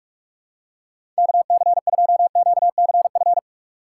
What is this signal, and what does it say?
Morse code, heard as a keyed tone, about 46 wpm, 700 Hz. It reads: KC2QCF
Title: USA Call Signs - 2x Character Spacing - 45wpm